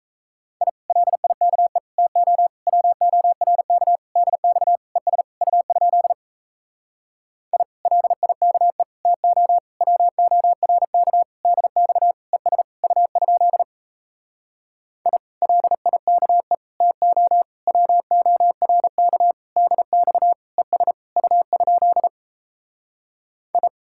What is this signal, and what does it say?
Morse code, heard as a keyed tone, about 37 wpm, 700 Hz. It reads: I LIKE TO WORK DX ES U? I LIKE TO WORK DX ES U? I LIKE TO WORK DX ES U? I